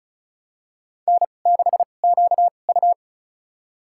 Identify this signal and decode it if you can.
Morse code, heard as a keyed tone, about 36 wpm, spaced as for 25 wpm, 700 Hz. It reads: N6QU
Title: USA Call Signs - 4 Characters - 2x Character Spacing - 35wpm